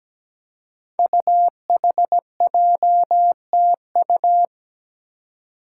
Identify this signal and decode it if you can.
Morse code, heard as a keyed tone, about 17 wpm, 700 Hz. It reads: UHJTU
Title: Five Challenging Letter-Number Combinations 17wpm